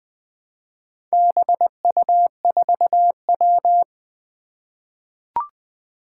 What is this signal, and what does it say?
Morse code, heard as a keyed tone, about 20 wpm, 700 Hz. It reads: BU4W
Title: Four Challenging Letter-Number Combinations 20wpm